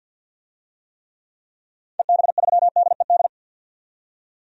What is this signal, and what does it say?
Morse code, heard as a keyed tone, about 53 wpm, 700 Hz. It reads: EB3DED
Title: DXCC-100 Call Signs - 6 Characters - 50wpm